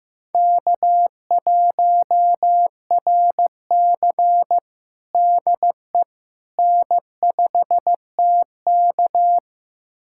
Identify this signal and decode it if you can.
Morse code, heard as a keyed tone, about 15 wpm, 700 Hz. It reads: K1RC DE N5TK
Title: USA Callsign DE Callsign - 4 Characters - 15wpm